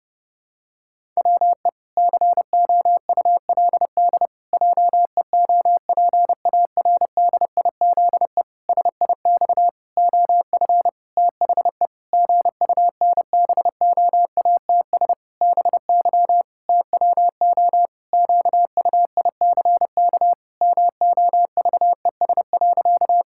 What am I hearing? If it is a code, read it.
Morse code, heard as a keyed tone, about 30 wpm, 700 Hz. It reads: WE COULD JEOPARDIZE SIX OF THE GUNBOATS BY TWO QUICK MOVES.
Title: Pangram Challenge v2 - 30wpm